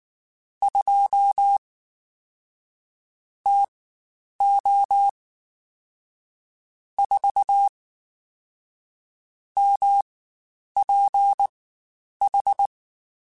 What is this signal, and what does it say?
Morse code, heard as a keyed tone, about 19 wpm, spaced as for 9 wpm, 800 Hz. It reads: 2 TO 4 MPH
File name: Spaced Characters / Spaced Characters-mp3s / Weather 4 14 sp @18